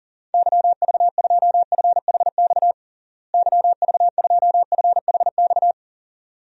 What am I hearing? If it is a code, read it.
Morse code, heard as a keyed tone, about 37 wpm, 700 Hz. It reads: YV2FHX YV2FHX
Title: DXCC-100 Call Signs - 6 Characters - Repeated 2x - 40wpm